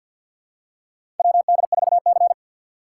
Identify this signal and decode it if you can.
Morse code, heard as a keyed tone, about 52 wpm, 700 Hz. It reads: WD4C
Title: USA Call Signs - 4 Characters - 2x Character Spacing - 50wpm